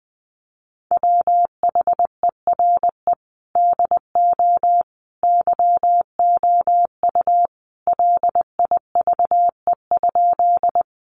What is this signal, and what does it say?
Morse code, heard as a keyed tone, about 20 wpm, 700 Hz. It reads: WHERE DO YOU LIVE?
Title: Sentences from Top 200 Words 20wpm